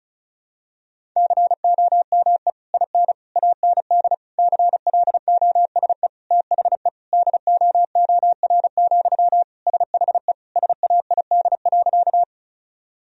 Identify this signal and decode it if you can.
Morse code, heard as a keyed tone, about 36 wpm, 700 Hz. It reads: COME IN AND CLOSE THE DOOR, SHE SAID.
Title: Sentences from Top 1000 Words - Review Part 1 - 35wpm